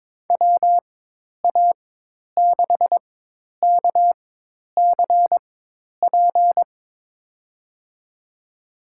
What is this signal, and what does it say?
Morse code, heard as a keyed tone, about 22 wpm, spaced as for 10 wpm, 700 Hz. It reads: WA6KCP
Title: USA Call Signs - 6 Characters - 4x Character Spacing - 22wpm